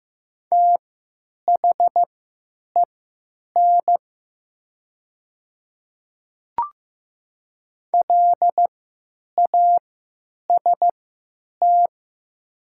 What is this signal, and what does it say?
Morse code, heard as a keyed tone, about 15 wpm, spaced as for 8 wpm, 700 Hz. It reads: THEN LAST
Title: 3x Character Spacing - Top 300 Words - 15wpm